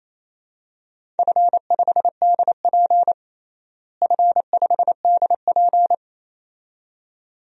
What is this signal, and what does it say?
Morse code, heard as a keyed tone, about 27 wpm, 700 Hz. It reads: F5DP F5DP
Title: DXCC-100 Call Signs - 4 Characters - Repeated 2x - 28wpm